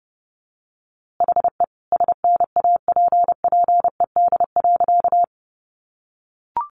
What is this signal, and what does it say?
Morse code, heard as a keyed tone, about 30 wpm, 700 Hz. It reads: HE SNAPPED.
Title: A Sound Of Thunder 30wpm